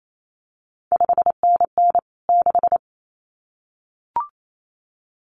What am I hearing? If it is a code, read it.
Morse code, heard as a keyed tone, about 28 wpm, 700 Hz. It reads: 5NN 6
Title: CQ WW DX Contest Exchange Encore - 28wpm